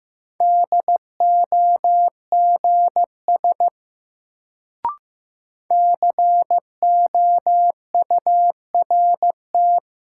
Morse code, heard as a keyed tone, about 15 wpm, 700 Hz. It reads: DOGS COURT
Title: Sentences from Top 1600 Words - New Word Drill - 15wpm